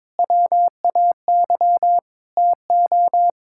Morse code, heard as a keyed tone, about 22 wpm, 700 Hz. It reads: WAY TO
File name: practice / 22wpm / Sets of 7 Words - 22wpm